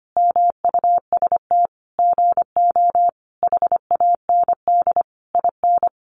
Morse code, heard as a keyed tone, about 25 wpm, 700 Hz. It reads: MUST GO HAND IN
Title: Sentences from Top 1200 Words 25wpm